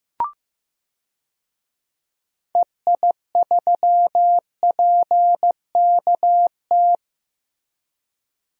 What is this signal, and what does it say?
Morse code, heard as a keyed tone, about 15 wpm, 700 Hz. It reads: EI3PKT
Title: DXCC-100 Call Signs - 6 Characters - Repeated 2x - 15wpm